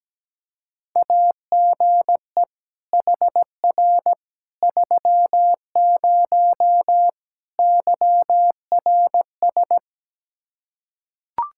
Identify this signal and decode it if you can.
Morse code, heard as a keyed tone, about 17 wpm, 700 Hz. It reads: AGE HR 30 YRS
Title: Practice QSO - Exchange 3 - More Info - 17wpm